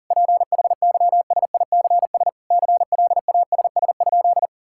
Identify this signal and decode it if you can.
Morse code, heard as a keyed tone, about 40 wpm, 700 Hz. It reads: PHYSICS CLASS?
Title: Sentences from Top 1000 Words - Review Part 5 - 40wpm